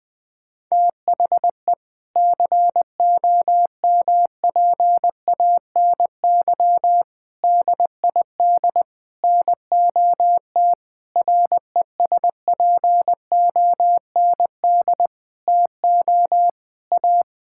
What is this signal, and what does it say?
Morse code, heard as a keyed tone, about 20 wpm, 700 Hz. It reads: THE COMPANY DID NOT RESPOND TO A
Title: Sentences with 8-Word Phrases - Spoken Beginning - 20wpm